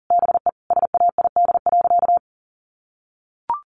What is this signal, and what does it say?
Morse code, heard as a keyed tone, about 37 wpm, 700 Hz. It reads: BE SAID.
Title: Sentences from Top 1000 Words 40wpm